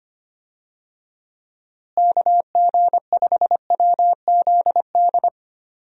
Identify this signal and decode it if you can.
Morse code, heard as a keyed tone, about 25 wpm, 700 Hz. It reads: KG5WZD
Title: USA Call Signs - 6 Characters - Repeated 2x - 25wpm